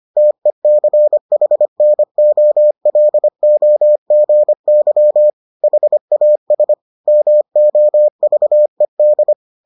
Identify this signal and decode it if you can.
Morse code, heard as a keyed tone, about 25 wpm, 600 Hz. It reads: TECHNOLOGY HAS MOVED